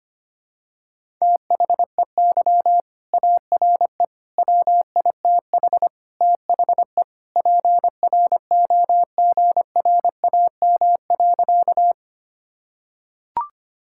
Morse code, heard as a keyed tone, about 25 wpm, 700 Hz. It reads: THEY ARE WITH THE PROGRAM.
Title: Sentences from Top 1000 Words - Review Part 5 - 25wpm